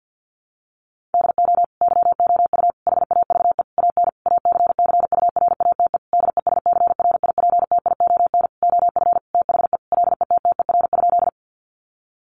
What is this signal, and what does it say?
Morse code, heard as a keyed tone, about 48 wpm, 700 Hz. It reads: DO YOU HAVE AN ACCURATE DESCRIPTION OF THE LETTER?
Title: Sentences from Top 1300 Words 50wpm